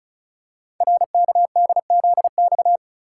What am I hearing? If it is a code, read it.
Morse code, heard as a keyed tone, about 36 wpm, 700 Hz. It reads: RKDZX
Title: Code Groups - 5 Letters - 35wpm